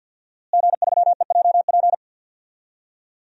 Morse code, heard as a keyed tone, about 48 wpm, 700 Hz. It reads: G3EJP
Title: DXCC-100 Call Signs - 5 Characters - Repeated 3x - 50wpm